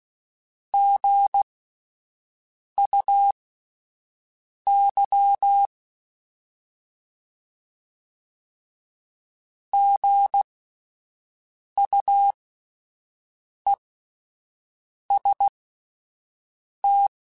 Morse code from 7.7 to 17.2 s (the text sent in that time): GUEST